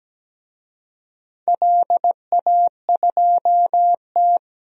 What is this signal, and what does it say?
Morse code, heard as a keyed tone, about 17 wpm, 700 Hz. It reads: LA2T
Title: DXCC-100 Call Signs - 4 Characters - 17wpm